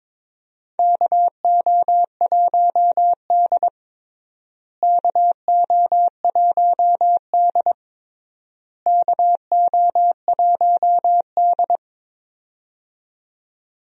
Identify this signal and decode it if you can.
Morse code, heard as a keyed tone, about 22 wpm, 700 Hz. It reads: KO1D KO1D KO1D
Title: USA Call Signs - 4 Characters - Repeated 3x - 22wpm